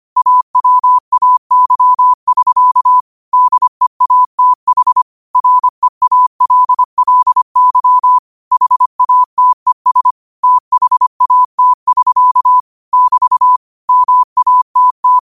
Morse code, heard as a keyed tone, about 25 wpm, 1000 Hz. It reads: AWAY<SK> DEATH REALLY HATES THAT<SK> <BT> MATT
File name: qotdfiles / 25wpm